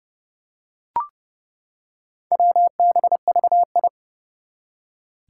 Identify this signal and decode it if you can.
Morse code, heard as a keyed tone, about 30 wpm, 700 Hz. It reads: WBVI
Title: Four Challenging Letter-Number Combinations 30wpm